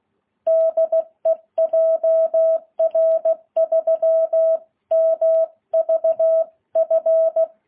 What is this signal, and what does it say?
Morse code, heard as a keyed tone, about 16 wpm, 660 Hz. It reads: DE JR3MVF
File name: jh8cbh